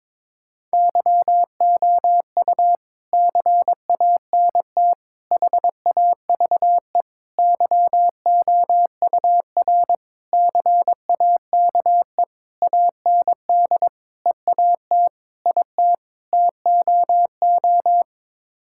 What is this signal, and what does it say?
Morse code, heard as a keyed tone, about 22 wpm, 700 Hz. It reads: YOU CANT HAVE YOUR CAKE AND EAT IT TOO
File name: Sets of 9 Words - 22wpm